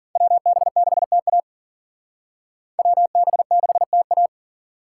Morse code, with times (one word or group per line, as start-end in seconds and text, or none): 0.2-1.4 WB6TA
2.8-4.3 WB6TA